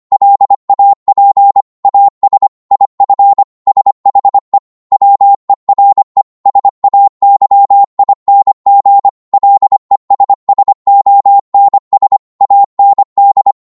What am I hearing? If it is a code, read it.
Morse code, heard as a keyed tone, about 25 wpm, 800 Hz. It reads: LAP AS IF SHE WERE SAYING LESSONS AND